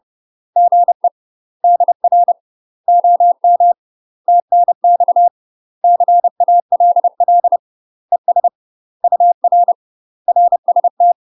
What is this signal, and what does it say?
Morse code, heard as a keyed tone, about 30 wpm, 700 Hz. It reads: GE DR OM TNX CALL ES UR RST
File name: UK MASTER QSOs TRAINING / 30WPM / UK MASTER QSOs 0019